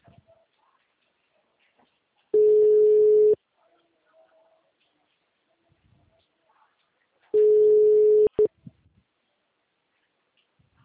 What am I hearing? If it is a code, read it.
Ringback tone (Europe, 425 Hz).